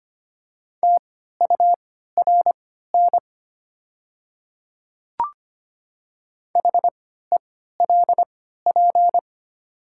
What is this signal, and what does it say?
Morse code, heard as a keyed tone, about 25 wpm, spaced as for 14 wpm, 700 Hz. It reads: TURN HELP
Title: 3x Character Spacing - Top 200 Words - 25wpm